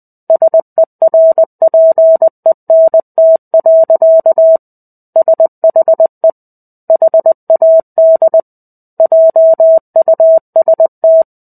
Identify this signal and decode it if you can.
Morse code, heard as a keyed tone, about 20 wpm, 650 Hz. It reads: SERPENT. SHE HAD JUST
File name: 05-Chapter-05 20 WPM